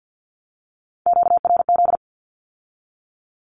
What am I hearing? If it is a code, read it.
Morse code, heard as a keyed tone, about 50 wpm, 700 Hz. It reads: QRZ
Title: Q-Codes with Spoken Meaning - 50wpm